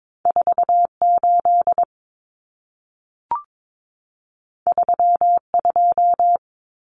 Morse code, heard as a keyed tone, about 22 wpm, 700 Hz. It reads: 48 32
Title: Common Cut Numbers - 2-Digit 22wpm